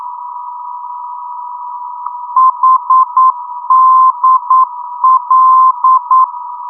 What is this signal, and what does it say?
Morse code, heard as a keyed tone, about 9 wpm, 1040 Hz. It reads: HDL
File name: europe / d-8060-cw-id-2004-rb